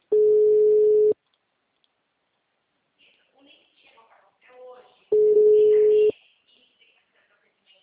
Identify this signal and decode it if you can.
Ringback tone (Europe, 425 Hz), with speech.